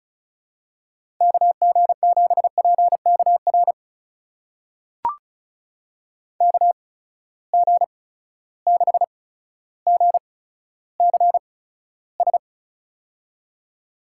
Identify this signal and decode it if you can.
Morse code, heard as a keyed tone, about 34 wpm, spaced as for 15 wpm, 700 Hz. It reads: KG7PKR KG6GCS
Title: USA Call Signs - 6 Characters - 8x Character Spacing - 35wpm